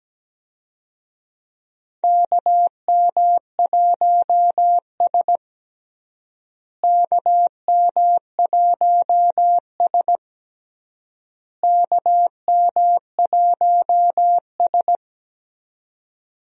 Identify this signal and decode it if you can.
Morse code, heard as a keyed tone, about 17 wpm, 700 Hz. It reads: KM1S KM1S KM1S
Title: USA Call Signs - 4 Characters - Repeated 3x - 17wpm